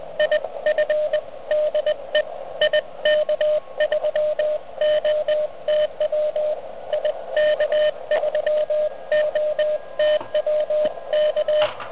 Morse code, heard as a keyed tone, about 20 wpm, 610 Hz, with radio noise. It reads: IF DE IK3OTW IK3OTW K